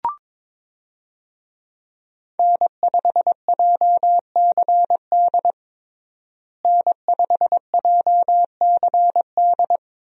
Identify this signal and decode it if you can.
Morse code, heard as a keyed tone, about 21 wpm, 700 Hz. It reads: N5JCD N5JCD
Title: USA Call Signs - 5 Characters - Repeated 2x - 22wpm